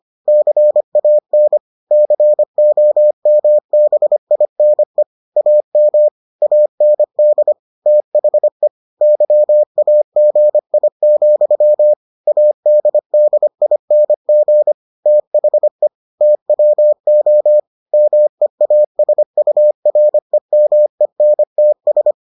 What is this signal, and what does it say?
Morse code, heard as a keyed tone, about 25 wpm, 600 Hz. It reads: CAN COMBINE AM AND THE YAGI, ADDING THE TWO MEASUREMENTS